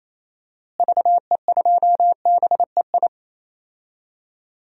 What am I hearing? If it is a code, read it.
Morse code, heard as a keyed tone, about 27 wpm, 700 Hz. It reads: VE2BEI
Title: DXCC-100 Call Signs - 6 Characters - Repeated 3x - 28wpm